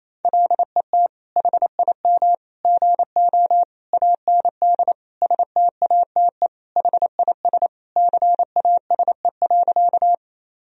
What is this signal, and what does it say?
Morse code, heard as a keyed tone, about 28 wpm, 700 Hz. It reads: LET HIM GO AND STATE HIS CASE.
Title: Sentences from Top 1000 Words - Review Part 2 - 28wpm